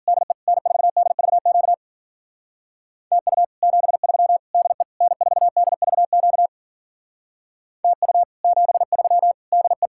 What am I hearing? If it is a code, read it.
Morse code, heard as a keyed tone, about 46 wpm, 700 Hz. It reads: DE N4DVQ TU 73 DE N4DVQ TU 73 DE
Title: Speed-Racing Common QSO Phrases - 40wpm